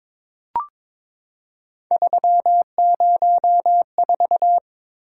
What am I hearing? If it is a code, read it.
Morse code, heard as a keyed tone, about 22 wpm, 700 Hz. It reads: 304
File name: practice / 22wpm / Three Letter-Number Combinations - 22wpm